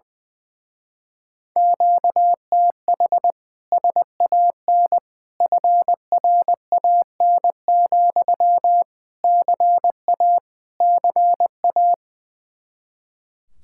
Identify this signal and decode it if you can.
Morse code, heard as a keyed tone, about 20 wpm, 700 Hz. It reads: QTH SAN FRAN, CA CA
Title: Speed-Racing Common QSO Phrases - 20wpm